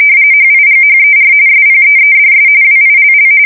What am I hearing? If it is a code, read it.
RTTY, about 45.45 baud. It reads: HIGHER SPEEDS WILL
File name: fsk170 / printing-telegraph